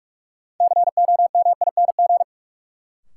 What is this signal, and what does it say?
Morse code, heard as a keyed tone, about 43 wpm, 700 Hz. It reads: COMING
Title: Sentences from Top 1300 Words - New Word Drill - 45wpm